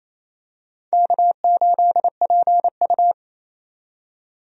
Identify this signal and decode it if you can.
Morse code, heard as a keyed tone, about 28 wpm, 700 Hz. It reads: K8PU
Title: USA Call Signs - 4 Characters - 28wpm